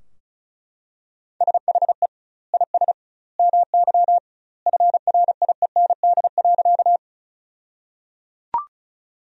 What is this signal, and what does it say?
Morse code, heard as a keyed tone, about 34 wpm, 700 Hz. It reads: SHE IS MY FRIEND.
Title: 2x Word-Spacing - Sentences from Top 300 Words - 35wpm